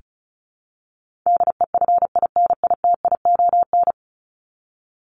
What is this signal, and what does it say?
Morse code, heard as a keyed tone, about 34 wpm, 700 Hz. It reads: DEFINITION
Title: Top 501-750 Words 35wpm